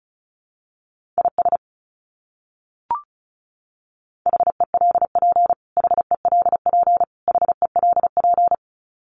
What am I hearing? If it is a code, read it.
Morse code, heard as a keyed tone, about 34 wpm, 700 Hz. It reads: IS HELP HELP HELP
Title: Top 200 Words Repeated 35wpm